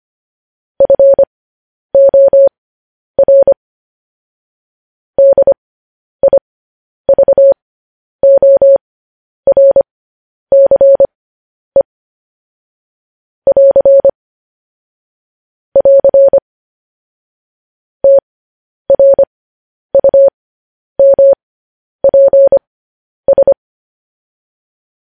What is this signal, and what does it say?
Morse code, heard as a keyed tone, about 25 wpm, spaced as for 10 wpm, 550 Hz. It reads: FOR DIVORCE <AR> <AR> TRUMPS
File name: podcast / News headlines-1x-25x10